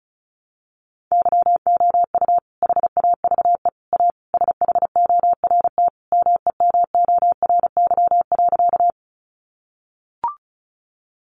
Morse code, heard as a keyed tone, about 34 wpm, 700 Hz. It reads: YOU HAVE A SHORT MEMORY.
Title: Sentences from Top 1300 Words 35wpm